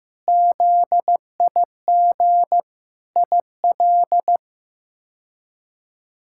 Morse code, heard as a keyed tone, about 15 wpm, 700 Hz. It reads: ZIG IL
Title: K1USN SST QSO Practice - 15wpm